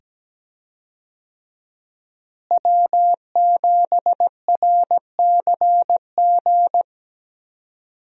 Morse code, heard as a keyed tone, about 17 wpm, 700 Hz. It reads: W7RCG
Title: USA Call Signs - 5 Characters - Repeated 2x - 17wpm